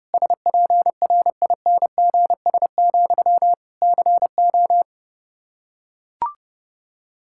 Morse code, heard as a keyed tone, about 30 wpm, 700 Hz. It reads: SPRINGS, CO
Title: Practice QSO - Exchange 1 - Beginning - 30wpm